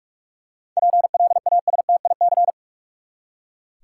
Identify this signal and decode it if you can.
Morse code, heard as a keyed tone, about 41 wpm, 700 Hz. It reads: PLASTIC
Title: Seven Letter Words - 45wpm